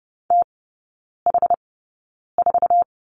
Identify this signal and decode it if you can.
Morse code, heard as a keyed tone, about 30 wpm, spaced as for 9 wpm, 700 Hz. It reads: TH4
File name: Lesson 18e - Letter m - Three Characters 30wpm